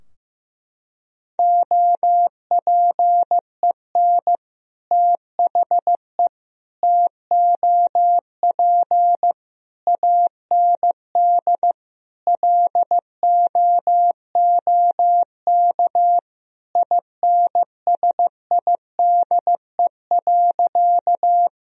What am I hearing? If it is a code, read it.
Morse code, heard as a keyed tone, about 15 wpm, 700 Hz. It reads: OPEN THE TOP AND LOOK INSIDE.
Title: Sentences from Top 1000 Words - Review Part 2 - 15wpm